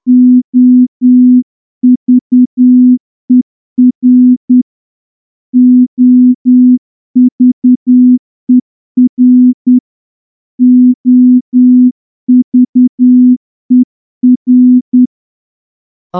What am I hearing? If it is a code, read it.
Morse code, heard as a keyed tone, about 10 wpm, 250 Hz. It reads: OVER OVER OVER
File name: morse-training-10WPM